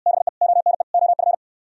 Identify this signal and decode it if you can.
Morse code, heard as a keyed tone, about 62 wpm, 700 Hz. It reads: BE ONE OF